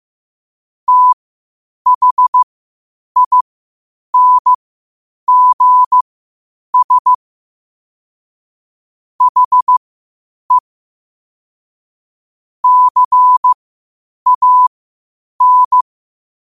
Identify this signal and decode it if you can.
Morse code, heard as a keyed tone, about 15 wpm, spaced as for 8 wpm, 1000 Hz. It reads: THINGS HE CAN
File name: qotdfiles / 8wpm